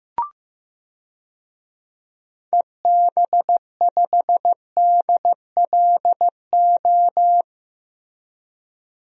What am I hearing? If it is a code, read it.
Morse code, heard as a keyed tone, about 15 wpm, 700 Hz. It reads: EB5DLO
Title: DXCC-100 Call Signs - 6 Characters - Repeated 2x - 15wpm